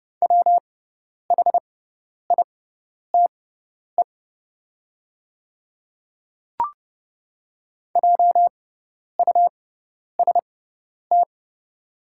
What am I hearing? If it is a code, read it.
Morse code, heard as a keyed tone, about 30 wpm, spaced as for 10 wpm, 700 Hz. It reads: WHITE JUST
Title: 6x Character Spacing - Top 300 Words - 30wpm